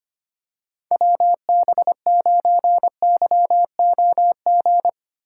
Morse code, heard as a keyed tone, about 25 wpm, 700 Hz. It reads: WB9YOG
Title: USA Call Signs - 6 Characters - 8x Character Spacing - 25wpm